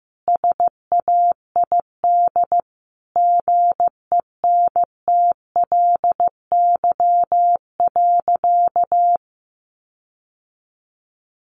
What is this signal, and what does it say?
Morse code, heard as a keyed tone, about 14 wpm, 700 Hz. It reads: SAID GENTLY.
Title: Sentences from Top 1500 Words 15wpm